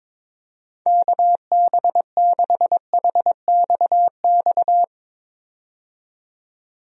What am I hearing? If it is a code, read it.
Morse code, heard as a keyed tone, about 22 wpm, 700 Hz. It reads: KB6HXX
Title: USA Call Signs - 6 Characters - 22wpm